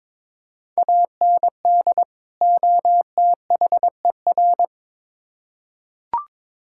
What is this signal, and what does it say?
Morse code, heard as a keyed tone, about 22 wpm, 700 Hz. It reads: AND OTHER
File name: Build A Phrase - 2-Word Phrases - 22wpm